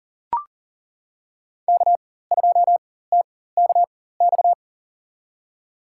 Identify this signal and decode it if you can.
Morse code, heard as a keyed tone, about 40 wpm, spaced as for 19 wpm, 700 Hz. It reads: K2TKX
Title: USA Call Signs - 5 Characters - 4x Character Spacing - 40wpm